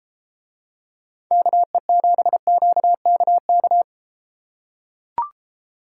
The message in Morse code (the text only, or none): KE7QKK